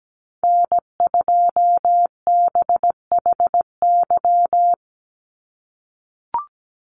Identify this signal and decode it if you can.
Morse code, heard as a keyed tone, about 17 wpm, 700 Hz. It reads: N2BHY
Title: USA Call Signs 17wpm